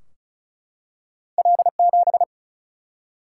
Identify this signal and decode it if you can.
Morse code, heard as a keyed tone, about 36 wpm, 700 Hz. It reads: L7
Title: Single-Letter Single-Number 35wpm